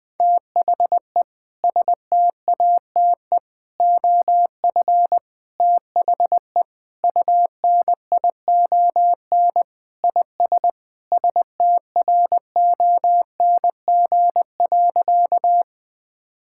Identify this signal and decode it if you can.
Morse code, heard as a keyed tone, about 20 wpm, 700 Hz. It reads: THE STATE OF THE UNION IS STRONG.
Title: Sentences with 3-Word Phrases - 20wpm